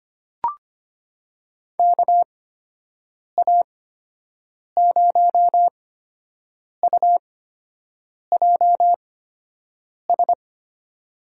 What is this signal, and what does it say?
Morse code, heard as a keyed tone, about 25 wpm, spaced as for 7 wpm, 700 Hz. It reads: KA0UJS
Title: USA Call Signs - 6 Characters - 8x Character Spacing - 25wpm